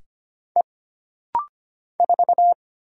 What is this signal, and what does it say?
Morse code, heard as a keyed tone, about 25 wpm, 700 Hz. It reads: E 4
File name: Single Character (Letter:Number:Punctuation:Prosign) - Rapid-Fire - 25wpm